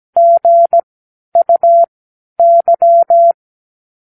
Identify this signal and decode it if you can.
Morse code, heard as a keyed tone, about 17 wpm, spaced as for 11 wpm, 680 Hz. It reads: GUY
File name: k7qo / Practicewords